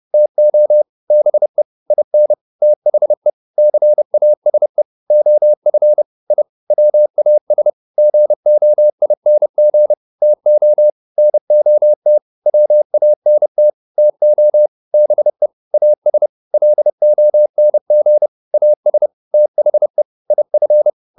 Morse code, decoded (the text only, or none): TO BE IN THE CASE OF I WAS GOING TO NOT WANT TO BE AS LONG AS THE IF